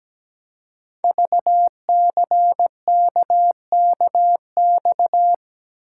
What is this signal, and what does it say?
Morse code, heard as a keyed tone, about 17 wpm, 700 Hz. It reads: VCKKX
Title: Code Groups - 5 Letters - 17wpm